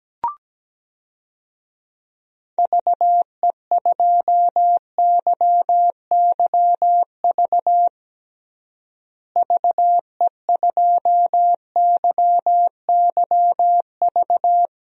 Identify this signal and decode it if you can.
Morse code, heard as a keyed tone, about 17 wpm, 700 Hz. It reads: VE2YYV VE2YYV
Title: DXCC-100 Call Signs - 6 Characters - Repeated 2x - 17wpm